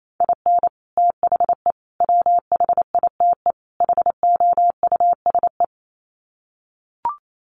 Morse code, heard as a keyed tone, about 28 wpm, 700 Hz. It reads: IN THE WHITE HOUSE
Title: Sets of 4 Words 28wpm